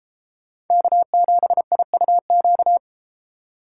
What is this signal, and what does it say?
Morse code, heard as a keyed tone, about 31 wpm, 700 Hz. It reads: K7IUQ